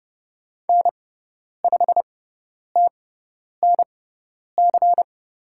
Morse code, heard as a keyed tone, about 30 wpm, spaced as for 10 wpm, 700 Hz. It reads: N5TNC
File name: Lesson 16g - Letter c - Call Signs 30wpm